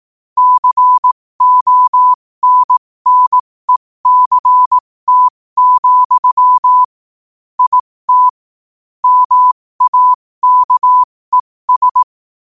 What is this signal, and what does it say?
Morse code, heard as a keyed tone, about 18 wpm, spaced as for 15 wpm, 1000 Hz. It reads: CONNECT, IT MAKES
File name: qotdfiles / 15wpm